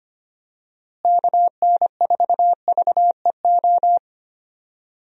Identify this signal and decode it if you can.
Morse code, heard as a keyed tone, about 25 wpm, 700 Hz. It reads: KN4VEO
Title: USA Call Signs - 6 Characters - Repeated 3x - 25wpm